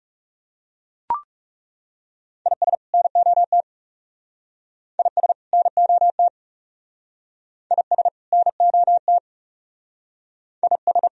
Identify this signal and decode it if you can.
Morse code, heard as a keyed tone, about 36 wpm, 700 Hz. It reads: IS NOT IS NOT IS NOT IS
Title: Speed-Racing - Sets of 2 Words 30wpm